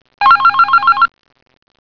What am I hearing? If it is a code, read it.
There is no code.